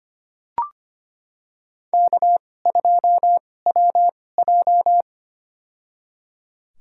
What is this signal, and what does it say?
Morse code, heard as a keyed tone, about 25 wpm, spaced as for 18 wpm, 700 Hz. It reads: K2WJ
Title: USA Call Signs - 4 Characters - 2x Character Spacing - 25wpm